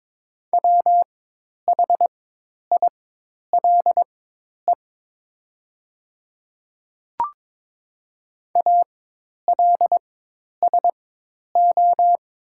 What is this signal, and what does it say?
Morse code, heard as a keyed tone, about 22 wpm, spaced as for 10 wpm, 700 Hz. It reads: WHILE ALSO
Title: 4x Character Spacing - Top 300 Words - 22wpm